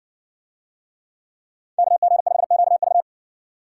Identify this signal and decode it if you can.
Morse code, heard as a keyed tone, about 60 wpm, 700 Hz. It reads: KG5QV